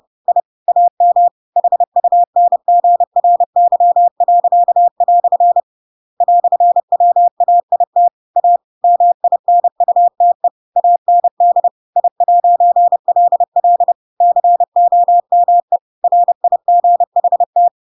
Morse code, heard as a keyed tone, about 30 wpm, 700 Hz. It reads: I AM HUNGRY." "WAIT A MINUTE AND I'LL COME RIGHT
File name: Pinocchio / Pinocchio ch6 30 30 ik1hge